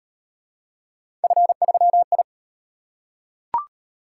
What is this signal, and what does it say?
Morse code, heard as a keyed tone, about 35 wpm, 700 Hz. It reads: F3I